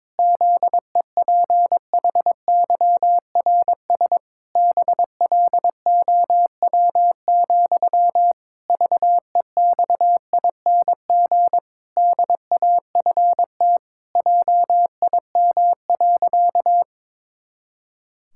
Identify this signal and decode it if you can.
Morse code, heard as a keyed tone, about 22 wpm, 700 Hz. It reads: ZEPHYRS BLOW, VEXING DAFT JIM.
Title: Pangram Challenge v2 - 22wpm